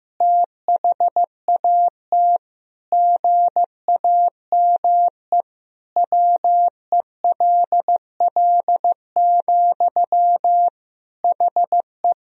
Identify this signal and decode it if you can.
Morse code, heard as a keyed tone, about 15 wpm, 700 Hz. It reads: THAT GAME WELL, HE
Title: Sentences from Top 1000 Words - Review Part 3 - 15wpm